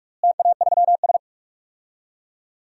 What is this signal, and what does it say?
Morse code, heard as a keyed tone, about 47 wpm, 700 Hz. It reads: TA3S
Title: DXCC-100 Call Signs - 4 Characters - Repeated 3x - 45wpm